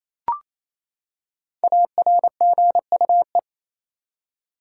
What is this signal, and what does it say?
Morse code, heard as a keyed tone, about 28 wpm, 700 Hz. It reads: ARGUE
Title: Five Letter Words - 28wpm